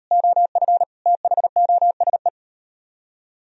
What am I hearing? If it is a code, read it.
Morse code, heard as a keyed tone, about 35 wpm, 700 Hz. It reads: OF THOSE